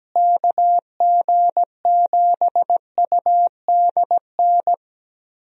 Morse code, heard as a keyed tone, about 17 wpm, 700 Hz. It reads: KG7UDN
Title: USA Call Signs - 6 Characters - Repeated 3x - 17wpm